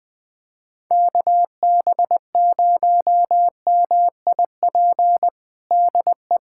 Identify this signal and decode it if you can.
Morse code, heard as a keyed tone, about 20 wpm, 700 Hz. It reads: KB0MIP DE
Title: USA Callsign DE Callsign 20wpm